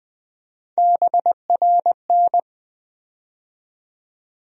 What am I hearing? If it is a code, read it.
Morse code, heard as a keyed tone, about 20 wpm, 700 Hz. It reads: BRN
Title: USA Call Sign Postfixes - 3 Letters 20wpm